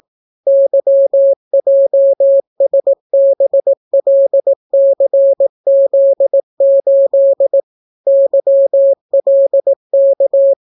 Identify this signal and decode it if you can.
Morse code, heard as a keyed tone, about 18 wpm, 550 Hz. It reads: YJSBLCZ8 YLK